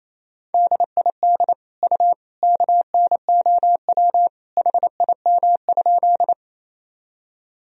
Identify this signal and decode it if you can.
Morse code, heard as a keyed tone, about 28 wpm, 700 Hz. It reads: DID U KNOW HIM?
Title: Sentences from Top 100 Words and Abbreviations 28wpm